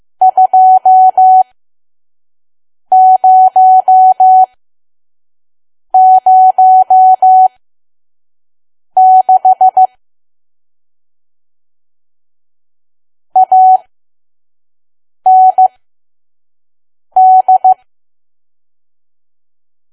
Morse code, heard as a keyed tone, about 15 wpm, spaced as for 5 wpm, 750 Hz. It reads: 2006 AND
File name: Archive / 5 WPM files / PM